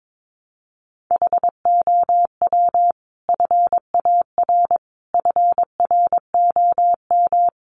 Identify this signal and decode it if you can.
Morse code, heard as a keyed tone, about 22 wpm, 700 Hz. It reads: HOW FAR FROM
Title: Sentences from Top 400 Words 22wpm